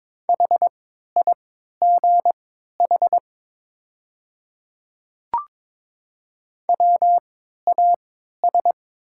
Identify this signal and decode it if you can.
Morse code, heard as a keyed tone, about 22 wpm, spaced as for 13 wpm, 700 Hz. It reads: HIGH WAS
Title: 3x Character Spacing - Top 200 Words - 22wpm